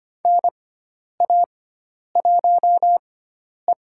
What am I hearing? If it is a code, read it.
Morse code, heard as a keyed tone, about 25 wpm, spaced as for 10 wpm, 700 Hz. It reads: NA1E
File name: Lesson 16g - Letter c - Call Signs 25wpm